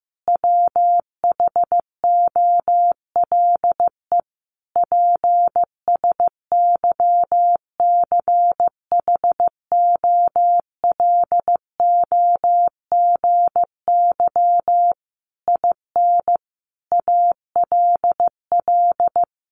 Morse code, heard as a keyed tone, about 15 wpm, 700 Hz. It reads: WHOLE PSYCHOLOGY IN ALL
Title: Sentences from Top 1400 Words 15wpm